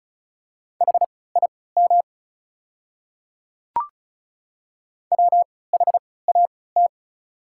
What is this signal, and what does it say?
Morse code, heard as a keyed tone, about 34 wpm, spaced as for 20 wpm, 690 Hz. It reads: HIM WHAT
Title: 3x Character Spacing - Top 100 Words - 35wpm